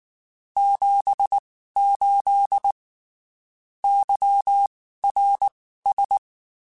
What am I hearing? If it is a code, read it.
Morse code, heard as a keyed tone, about 19 wpm, spaced as for 14 wpm, 800 Hz. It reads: 78 YRS
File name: (spaced) / QSOs Spaced-mp3s / WA7QOX 2 @18 sp